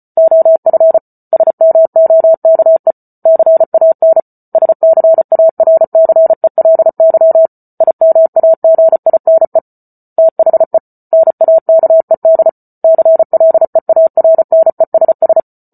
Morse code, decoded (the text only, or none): OF SMOKE CAN SCARCELY IMAGINE THE NAKED CLEARNESS